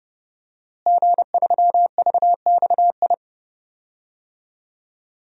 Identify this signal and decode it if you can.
Morse code, heard as a keyed tone, about 32 wpm, 700 Hz. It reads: G3VXI
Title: DXCC-100 Call Signs - 5 Characters - Repeated 2x - 30wpm